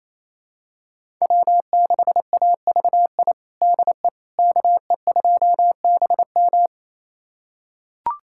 Morse code, heard as a keyed tone, about 27 wpm, 700 Hz. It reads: W6AVI DE KE2BM
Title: USA Callsign DE Callsign - 5 Characters - 28wpm